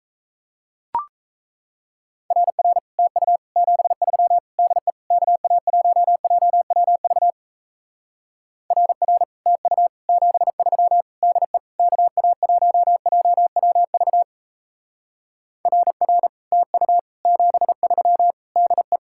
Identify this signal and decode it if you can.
Morse code, heard as a keyed tone, about 35 wpm, 700 Hz. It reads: RR TU 73 DE KA1JWV RR TU 73 DE KA1JWV RR TU 73 DE